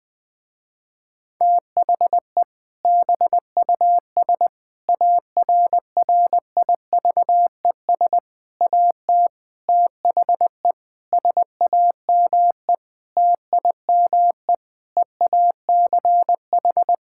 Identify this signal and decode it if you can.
Morse code, heard as a keyed tone, about 20 wpm, 700 Hz. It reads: THE BUS ARRIVES AT THE SAME TIME EACH
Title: Difficult Sentences with Hints - 20wpm